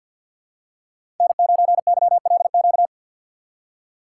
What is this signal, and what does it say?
Morse code, heard as a keyed tone, about 53 wpm, 700 Hz. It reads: N9YLQ
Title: USA Call Signs - 5 Characters - 4x Character Spacing - 50wpm